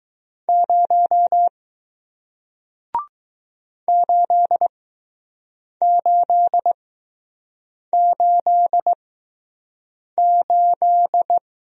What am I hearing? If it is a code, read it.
Morse code, heard as a keyed tone, about 20 wpm, spaced as for 6 wpm, 700 Hz. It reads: 0 8888